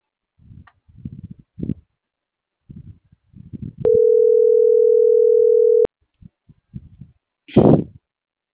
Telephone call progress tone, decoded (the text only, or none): ringback tone